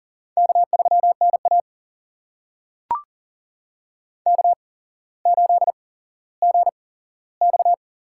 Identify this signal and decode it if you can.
Morse code, heard as a keyed tone, about 40 wpm, spaced as for 17 wpm, 700 Hz. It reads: K3NA K8GX
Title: USA Call Signs - 8x Character Spacing - 40wpm